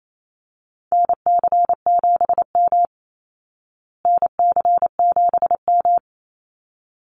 Morse code, heard as a keyed tone, about 28 wpm, 700 Hz. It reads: NC7M NC7M
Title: Call Signs Repeated 3x - 28wpm